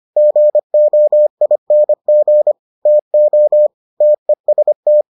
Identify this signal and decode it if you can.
Morse code, heard as a keyed tone, about 25 wpm, 600 Hz. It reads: GOING TO TEST